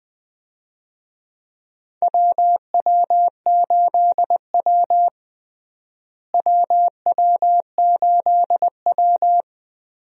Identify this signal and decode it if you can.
Morse code, heard as a keyed tone, about 20 wpm, 700 Hz. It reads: WW8W WW8W
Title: USA Call Signs - 4 Characters - Repeated 3x - 20wpm